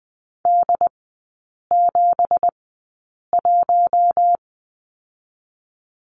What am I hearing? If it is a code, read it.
Morse code, heard as a keyed tone, about 20 wpm, spaced as for 8 wpm, 700 Hz. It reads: D71
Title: 166 Lesson 27e - Number 7 - Three Characters 20wpm